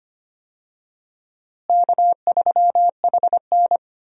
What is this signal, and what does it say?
Morse code, heard as a keyed tone, about 25 wpm, 700 Hz. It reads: K3HN
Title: USA Call Signs - 4 Characters - Repeated 3x - 25wpm